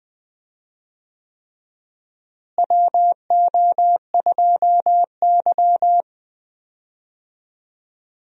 Morse code, heard as a keyed tone, about 20 wpm, 700 Hz. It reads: WO2Y
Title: USA Call Signs - 4 Characters - Repeated 3x - 20wpm